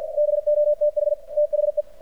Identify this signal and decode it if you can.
Morse code, heard as a keyed tone, about 57 wpm, 600 Hz. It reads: BOTH THE
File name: archived / CRSnet